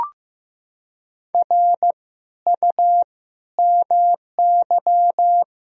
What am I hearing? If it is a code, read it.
Morse code, heard as a keyed tone, about 15 wpm, 700 Hz. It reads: R U MY
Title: Sentences from Top 300 Words and Abbreviations 15wpm